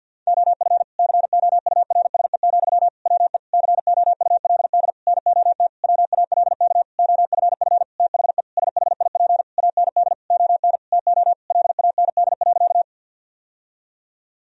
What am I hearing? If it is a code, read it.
Morse code, heard as a keyed tone, about 52 wpm, 700 Hz. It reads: OF COURSE, WE COULD NOT WALK OFF THE SHIP AND ON TO LAND.
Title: Sentences from Top 1000 Words - Review Part 2 - 50wpm